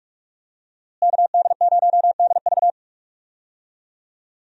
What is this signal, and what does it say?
Morse code, heard as a keyed tone, about 41 wpm, 700 Hz. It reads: KD0DV
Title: USA Call Signs - 5 Characters - Repeated 3x - 45wpm